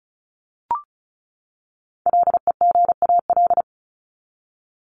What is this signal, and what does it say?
Morse code, heard as a keyed tone, about 34 wpm, 700 Hz. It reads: LEGAL
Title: Top 1001-1250 Words 35wpm